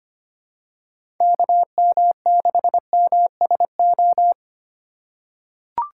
Morse code, heard as a keyed tone, about 25 wpm, 700 Hz. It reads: KM6MSO
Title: USA Call Signs - 6 Characters - 4x Character Spacing - 25wpm